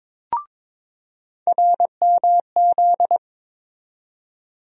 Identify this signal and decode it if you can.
Morse code, heard as a keyed tone, about 21 wpm, 700 Hz. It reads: RMZ
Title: USA Call Sign Postfixes - 3 Letters 22wpm